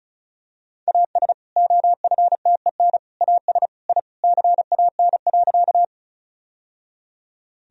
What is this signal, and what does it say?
Morse code, heard as a keyed tone, about 36 wpm, 700 Hz. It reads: AS OFTEN AS I CAN.
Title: Sentences from Top 1000 Words - Review Part 1 - 35wpm